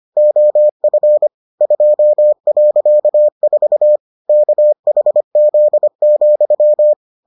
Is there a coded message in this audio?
Yes: Morse code from 0.2 to 6.9 s, heard as a keyed tone, about 25 wpm, 600 Hz. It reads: OF 2.4 KHZ,